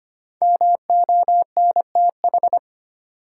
Morse code, heard as a keyed tone, about 25 wpm, 700 Hz. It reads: MONTH
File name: 5 Letter Words - First 3 Letters Given - 25wpm